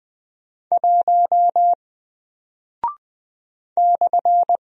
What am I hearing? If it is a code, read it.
Morse code, heard as a keyed tone, about 20 wpm, 700 Hz. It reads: 1 /
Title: Single Character (Letter:Number:Punctuation:Prosign) Alt 20wpm